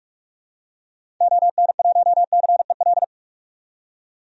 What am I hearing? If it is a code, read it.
Morse code, heard as a keyed tone, about 46 wpm, 700 Hz. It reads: ON1CEL
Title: DXCC-100 Call Signs - 6 Characters - 45wpm